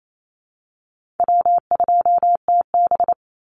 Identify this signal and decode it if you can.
Morse code, heard as a keyed tone, about 28 wpm, 700 Hz. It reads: W2TB
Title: Call Signs Repeated 3x - 28wpm